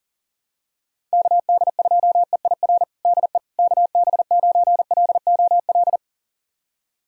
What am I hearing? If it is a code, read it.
Morse code, heard as a keyed tone, about 40 wpm, 700 Hz. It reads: KD2EIR DE KB9LOL
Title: USA Callsign DE Callsign - 6 Characters - 40wpm